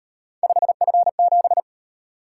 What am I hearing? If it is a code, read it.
Morse code, heard as a keyed tone, about 38 wpm, 700 Hz. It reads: 5F7